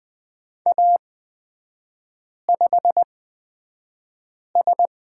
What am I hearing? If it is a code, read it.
Morse code, heard as a keyed tone, about 20 wpm, spaced as for 5 wpm, 700 Hz. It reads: A5S
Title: 224 Lesson 36e - BT - Three Characters 20wpm